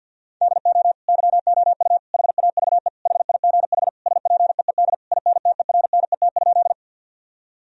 Morse code, heard as a keyed tone, about 50 wpm, 700 Hz. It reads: DO YOU HAVE HIGH SPEED INTERNET?
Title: Sentences from Top 1000 Words - Review Part 4 - 50wpm